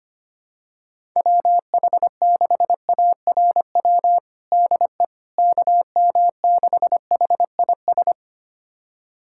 Morse code, heard as a keyed tone, about 25 wpm, 700 Hz. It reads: WH6ARW DE KM6HIS
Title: USA Callsign DE Callsign - 6 Characters - 25wpm